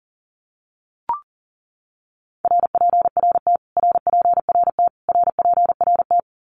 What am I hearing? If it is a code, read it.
Morse code, heard as a keyed tone, about 40 wpm, 700 Hz. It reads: RPRT RPRT RPRT
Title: Abbreviations Repeated 3x - 40wpm